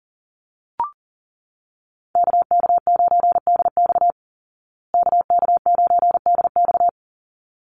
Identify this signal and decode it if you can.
Morse code, heard as a keyed tone, about 40 wpm, 700 Hz. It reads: KK9DX KK9DX
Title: Call Signs Repeated 3x - 40wpm